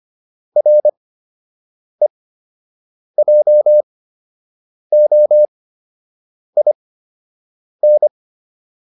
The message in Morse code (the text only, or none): REJOIN